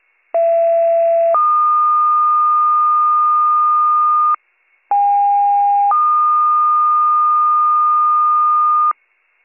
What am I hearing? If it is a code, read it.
Two-tone page, heard as a one-second tone, then a three-second tone, about 670 then 1220 Hz, about 810 then 1220 Hz. It is heard from 0.3 to 8.9 s.